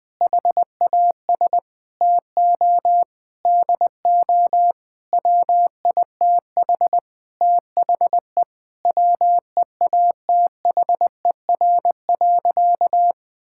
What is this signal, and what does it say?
Morse code, heard as a keyed tone, about 21 wpm, 700 Hz. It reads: HAS TO DO WITH THE WEATHER.
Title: Sentences with 9-Word Phrases - Spoken Beginning - 20wpm